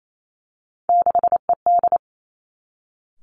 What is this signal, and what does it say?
Morse code, heard as a keyed tone, about 28 wpm, 700 Hz. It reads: 6ED
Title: Three Challenging Letter-Number Combinations 28wpm